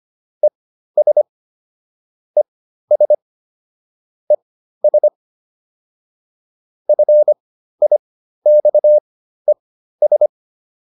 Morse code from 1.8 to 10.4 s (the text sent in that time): ES ES FIXES